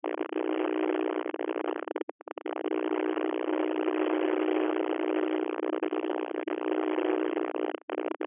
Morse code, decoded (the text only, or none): none